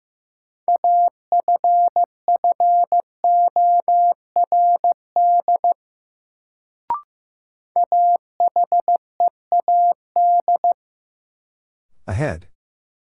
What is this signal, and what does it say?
Morse code, heard as a keyed tone, about 15 wpm, 700 Hz. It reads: AFFORD AHEAD
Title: Sentences from Top 1100 Words - New Word Drill Alt - 15wpm